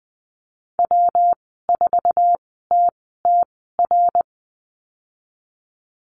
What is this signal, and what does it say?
Morse code, heard as a keyed tone, about 20 wpm, spaced as for 15 wpm, 700 Hz. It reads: W4TTR
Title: 147 Lesson 24g - Letter y - Call Signs 20wpm